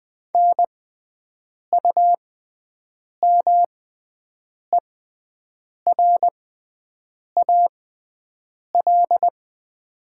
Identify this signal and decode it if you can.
Morse code, heard as a keyed tone, about 20 wpm, spaced as for 7 wpm, 700 Hz. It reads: NUMERAL
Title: 6x Character Spacing - Top 500 Words - 20wpm